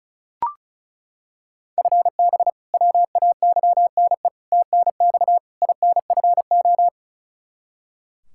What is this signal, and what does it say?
Morse code, heard as a keyed tone, about 36 wpm, 700 Hz. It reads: FB WAYNE TNX INFO
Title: Practice QSO - Exchange 3 - More Info - 35wpm